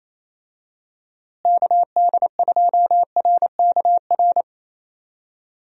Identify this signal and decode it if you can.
Morse code, heard as a keyed tone, about 28 wpm, 700 Hz. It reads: KD2RKR
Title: USA Call Signs - 6 Characters - Repeated 2x - 28wpm